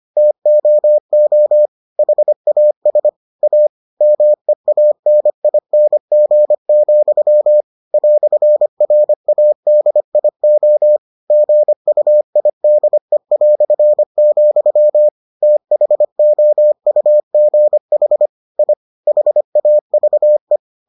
Morse code, heard as a keyed tone, about 25 wpm, 600 Hz. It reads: TOO HAS A MEANING, "RADIO GUIDE", THOUGH I HAVE